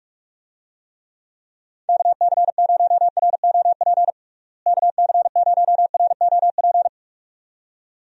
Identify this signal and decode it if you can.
Morse code, heard as a keyed tone, about 41 wpm, 700 Hz. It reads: KC0ROP KC0ROP
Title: USA Call Signs - 6 Characters - Repeated 2x - 45wpm